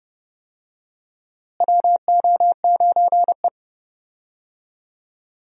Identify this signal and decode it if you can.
Morse code, heard as a keyed tone, about 30 wpm, 700 Hz. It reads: WO9E
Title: USA Call Signs - 4 Characters - Repeated 2x - 30wpm